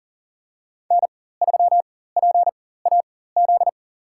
Morse code, heard as a keyed tone, about 37 wpm, spaced as for 18 wpm, 700 Hz. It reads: N3PAZ
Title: USA Call Signs - 5 Characters - 4x Character Spacing - 40wpm